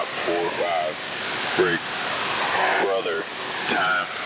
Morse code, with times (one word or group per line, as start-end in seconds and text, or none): none